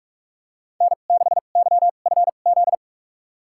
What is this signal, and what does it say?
Morse code, heard as a keyed tone, about 41 wpm, spaced as for 31 wpm, 700 Hz. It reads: N6YFZ
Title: USA Call Signs - 5 Characters - 2x Character Spacing - 45wpm